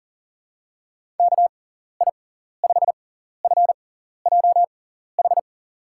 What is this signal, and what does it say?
Morse code, heard as a keyed tone, about 37 wpm, spaced as for 14 wpm, 700 Hz. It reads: KI5FJH
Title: USA Call Signs - 6 Characters - 6x Character Spacing - 40wpm